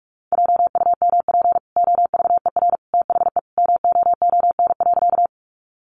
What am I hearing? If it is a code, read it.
Morse code, heard as a keyed tone, about 44 wpm, 700 Hz. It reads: JUMP OVER THE MOON.
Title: Sentences from Top 1300 Words 45wpm